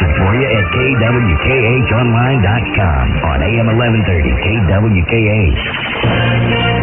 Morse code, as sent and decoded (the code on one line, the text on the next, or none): none